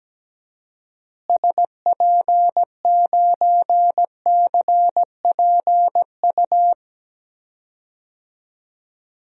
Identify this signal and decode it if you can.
Morse code, heard as a keyed tone, about 17 wpm, 700 Hz. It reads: SP9CPU
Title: DXCC-100 Call Signs - 6 Characters - Repeated 3x - 17wpm